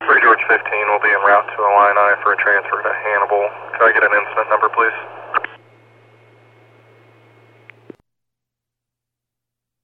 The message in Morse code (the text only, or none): none